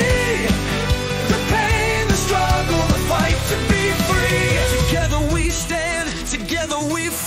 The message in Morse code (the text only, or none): none